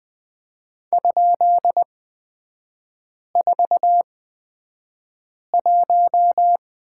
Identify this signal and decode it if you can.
Morse code, heard as a keyed tone, about 20 wpm, spaced as for 5 wpm, 700 Hz. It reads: ?41
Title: 250 Lesson 40e - BK - Three Characters 20wpm